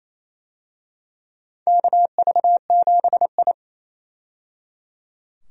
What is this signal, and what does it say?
Morse code, heard as a keyed tone, about 28 wpm, 700 Hz. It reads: KV7I
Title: USA Call Signs - 4 Characters - 28wpm